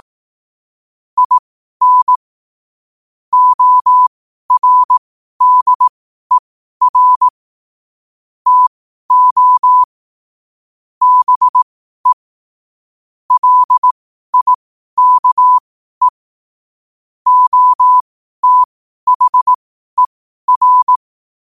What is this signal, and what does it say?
Morse code, heard as a keyed tone, about 18 wpm, spaced as for 12 wpm, 1000 Hz. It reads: IN ORDER TO BE LIKE OTHER
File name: qotdfiles / 12wpm